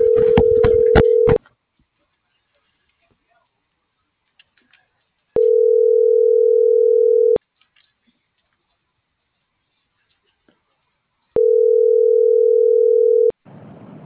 Ringback tone (North America), 440 + 480 Hz.